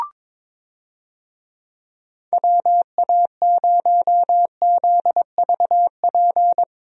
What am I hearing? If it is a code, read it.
Morse code, heard as a keyed tone, about 22 wpm, 700 Hz. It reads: WA0ZVP
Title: USA Call Signs - 6 Characters - Repeated 2x - 22wpm